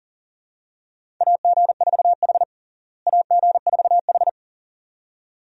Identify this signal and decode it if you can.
Morse code, heard as a keyed tone, about 40 wpm, 700 Hz. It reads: AG4H AG4H
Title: USA Call Signs - 4 Characters - Repeated 2x - 40wpm